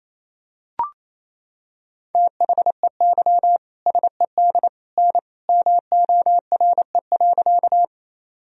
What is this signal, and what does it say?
Morse code, heard as a keyed tone, about 28 wpm, 700 Hz. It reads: THEY SED N MORE.
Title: Sentences from Top 100 Words and Abbreviations 28wpm